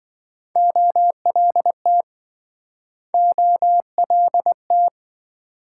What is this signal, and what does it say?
Morse code, heard as a keyed tone, about 27 wpm, 700 Hz. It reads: OLT OLT